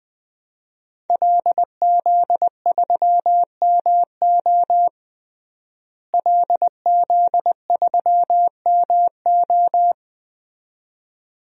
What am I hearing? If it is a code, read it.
Morse code, heard as a keyed tone, about 20 wpm, 700 Hz. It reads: LZ3MO LZ3MO
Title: DXCC-100 Call Signs - 5 Characters - Repeated 2x - 20wpm